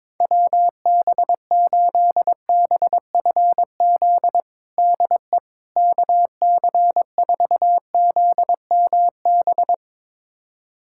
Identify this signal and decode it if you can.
Morse code, heard as a keyed tone, about 22 wpm, 700 Hz. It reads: WB8BFZ DE KC4ZMB
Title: USA Callsign DE Callsign - 6 Characters - 22wpm